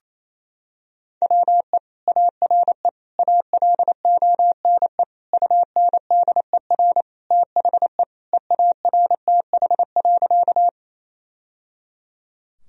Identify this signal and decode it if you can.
Morse code, heard as a keyed tone, about 28 wpm, 700 Hz. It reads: WE ARE ALONE UNDER THE EARTH.
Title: Sentences from Top 1000 Words - Review Part 1 - 28wpm